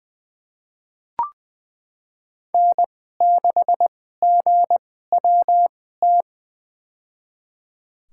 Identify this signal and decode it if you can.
Morse code, heard as a keyed tone, about 20 wpm, spaced as for 14 wpm, 700 Hz. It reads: N6GWT
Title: USA Call Signs - 5 Characters - 2x Character Spacing - 20wpm